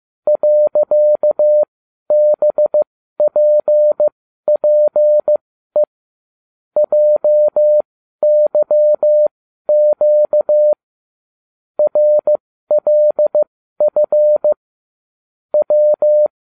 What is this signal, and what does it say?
Morse code, heard as a keyed tone, about 15 wpm, spaced as for 12 wpm, 600 Hz. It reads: .BPPE JYQ RLF W